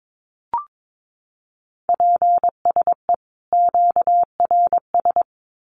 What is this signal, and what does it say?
Morse code, heard as a keyed tone, about 22 wpm, 700 Hz. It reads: PSE QRS
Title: Common QSO Phrases 22wpm